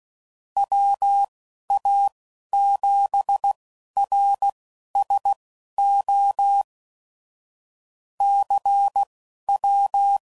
Morse code, heard as a keyed tone, about 16 wpm, spaced as for 11 wpm, 800 Hz. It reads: WA7RSO CW
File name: QST de WA7RSO